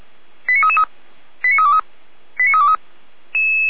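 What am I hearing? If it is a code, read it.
Five-tone selective call (ZVEI), heard as quick runs of five tones, tones of about 70 ms. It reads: 89333 89323 89323
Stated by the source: Received on an SDR; ZVEI five-tone selective call.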